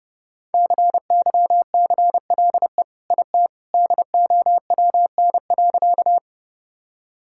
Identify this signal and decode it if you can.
Morse code, heard as a keyed tone, about 32 wpm, 700 Hz. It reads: CYCLE IT DOWN.
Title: Sentences from Top 1000 Words - Review Part 6 - 30wpm